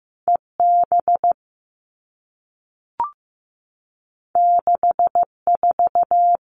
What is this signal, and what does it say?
Morse code, heard as a keyed tone, about 15 wpm, 700 Hz. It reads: EB 64
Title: All Cut Numbers - 2-Digit 15wpm